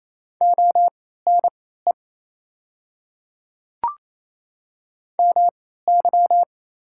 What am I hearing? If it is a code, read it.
Morse code, heard as a keyed tone, about 28 wpm, spaced as for 16 wpm, 690 Hz. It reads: ONE MY
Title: 3x Character Spacing - Top 100 Words - 28wpm